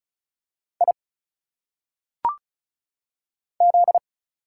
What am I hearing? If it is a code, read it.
Morse code, heard as a keyed tone, about 36 wpm, 700 Hz. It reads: I Z
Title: Single Letters - Phonetic Pronunciation 35wpm